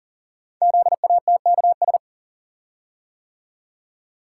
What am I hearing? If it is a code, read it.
Morse code, heard as a keyed tone, about 40 wpm, 700 Hz. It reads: ZATKS
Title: Code Groups - 5 Letters - 40wpm